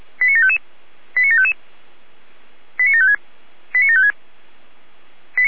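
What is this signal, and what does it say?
Five-tone selective call (ZVEI), heard as quick runs of five tones, tones of about 70 ms. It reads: 89755 89755 89756 89756
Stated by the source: ZVEI five-tone selective call received with an SDR.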